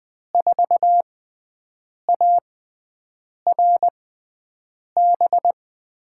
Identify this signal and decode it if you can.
Morse code, heard as a keyed tone, about 20 wpm, spaced as for 7 wpm, 700 Hz. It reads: 4ARB
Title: USA Call Signs - 6 Characters - 6x Character Spacing - 20wpm